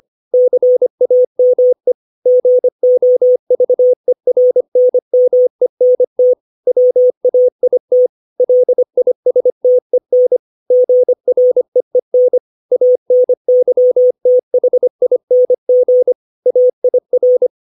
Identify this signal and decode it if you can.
Morse code, heard as a keyed tone, about 25 wpm, 500 Hz. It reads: CAME GOVERNMENT WAIT LISTEN GREEN ANYTHING AIR
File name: 632 words / 632 Words 25WPM